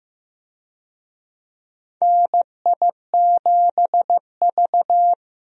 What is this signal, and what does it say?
Morse code, heard as a keyed tone, about 15 wpm, 700 Hz. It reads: NI7V
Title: USA Call Signs - 4 Characters - Repeated 3x - 15wpm